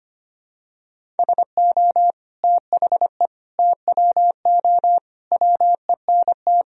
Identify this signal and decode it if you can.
Morse code, heard as a keyed tone, about 25 wpm, 700 Hz. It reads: SO THE TWO WENT
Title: Sentences from Top 1000 Words - Review Part 1 - 25wpm